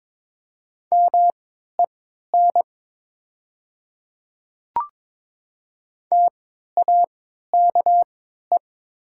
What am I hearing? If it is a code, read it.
Morse code, heard as a keyed tone, about 22 wpm, spaced as for 12 wpm, 700 Hz. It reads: MEN TAKE
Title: 3x Character Spacing - Top 200 Words - 22wpm